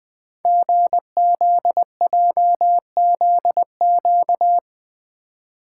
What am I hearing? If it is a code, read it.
Morse code, heard as a keyed tone, about 20 wpm, 700 Hz. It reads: GZJZQ
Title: Code Groups - 5 Letter:Numbers - 20wpm